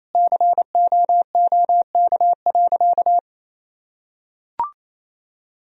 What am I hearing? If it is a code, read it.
Morse code, heard as a keyed tone, about 28 wpm, 700 Hz. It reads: COOK.
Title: Sentences from Top 1000 Words - Review Part 4 - 28wpm